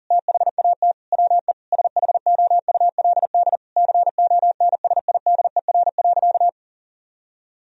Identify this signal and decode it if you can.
Morse code, heard as a keyed tone, about 37 wpm, 700 Hz. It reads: THAT WE SHOULD CONSIDER.
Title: Sentences from Top 1000 Words - Review Part 4 - 40wpm